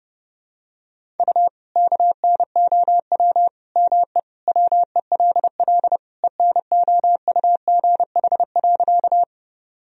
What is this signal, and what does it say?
Morse code, heard as a keyed tone, about 30 wpm, 700 Hz. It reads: U KNOW ME WELL ENOUGH.
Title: Sentences from Top 400 Words and Abbreviations 30wpm